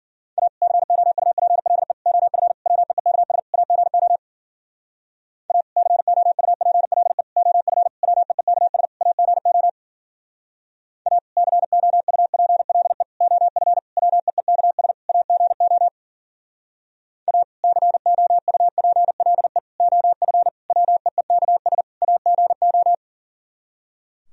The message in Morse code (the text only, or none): A COUPLE OF WEEKS AGO A COUPLE OF WEEKS AGO A COUPLE OF WEEKS AGO A COUPLE OF WEEKS AGO